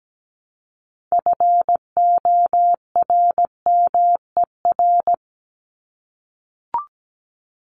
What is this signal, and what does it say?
Morse code, heard as a keyed tone, about 17 wpm, 700 Hz. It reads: FORMER
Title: LY Suffix - Top 100 Words with Base Words 17wpm